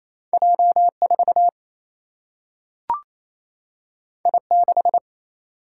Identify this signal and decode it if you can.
Morse code, heard as a keyed tone, about 28 wpm, 700 Hz. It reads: J4 I6
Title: Single-Letter Single-Number 28wpm